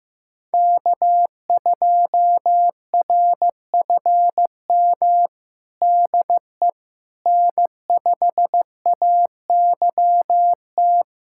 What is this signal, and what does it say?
Morse code, heard as a keyed tone, about 15 wpm, 700 Hz. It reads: K2RFM DE N5AYT
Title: USA Callsign DE Callsign - 5 Characters - 15wpm